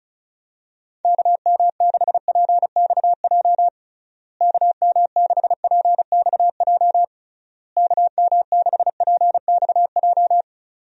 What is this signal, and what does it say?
Morse code, heard as a keyed tone, about 34 wpm, 700 Hz. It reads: KM6PXJ KM6PXJ KM6PXJ
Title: USA Call Signs - 6 Characters - Repeated 3x - 35wpm